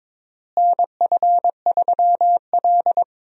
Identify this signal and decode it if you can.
Morse code, heard as a keyed tone, about 22 wpm, 700 Hz. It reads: NF3L
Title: USA Call Signs - 4 Characters - Repeated 2x - 22wpm